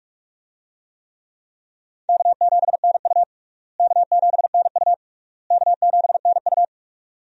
Morse code, heard as a keyed tone, about 46 wpm, 700 Hz. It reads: K7NU K7NU K7NU
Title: USA Call Signs - 4 Characters - Repeated 3x - 45wpm